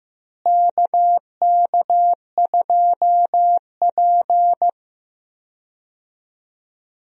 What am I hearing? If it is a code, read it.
Morse code, heard as a keyed tone, about 15 wpm, 700 Hz. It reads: KK2P
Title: USA Call Signs - 4 Characters - Repeated 3x - 15wpm